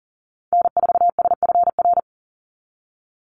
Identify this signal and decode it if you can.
Morse code, heard as a keyed tone, about 40 wpm, 700 Hz. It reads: N4SFR
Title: NA SOTA Activator Callsigns 40wpm